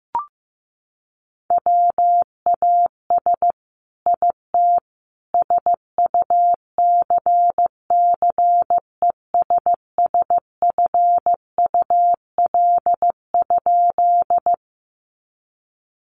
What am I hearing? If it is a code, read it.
Morse code, heard as a keyed tone, about 15 wpm, 700 Hz. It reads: WAS IT SUCCESSFUL?
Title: A Sound Of Thunder 15wpm